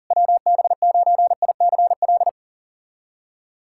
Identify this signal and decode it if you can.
Morse code, heard as a keyed tone, about 44 wpm, 700 Hz. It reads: WB9ICL
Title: SOTA QSO Practice 40wpm